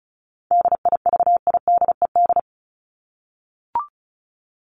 Morse code, heard as a keyed tone, about 34 wpm, 700 Hz. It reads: DIVIDED
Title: ED Suffix - Top 100 Words with Base Words 35wpm